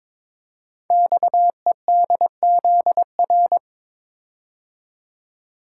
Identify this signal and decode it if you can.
Morse code, heard as a keyed tone, about 22 wpm, 700 Hz. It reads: XEDZR
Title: Code Groups - 5 Letters - 22wpm